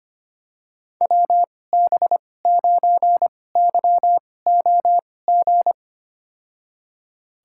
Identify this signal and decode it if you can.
Morse code, heard as a keyed tone, about 25 wpm, spaced as for 18 wpm, 700 Hz. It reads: WB9YOG
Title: USA Call Signs - 6 Characters - 2x Character Spacing - 25wpm